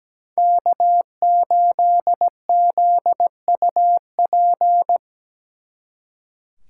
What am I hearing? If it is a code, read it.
Morse code, heard as a keyed tone, about 19 wpm, 700 Hz. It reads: K8ZUP
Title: Speed Racing - USA Call Signs 17wpm